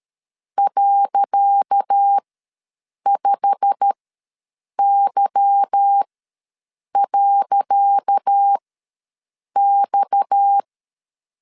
Morse code, heard as a keyed tone, about 13 wpm, spaced as for 7 wpm, 800 Hz. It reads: .5Y.X
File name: LEZIONI / 10 SIGNS / 10C DOT ws 12wpm 3548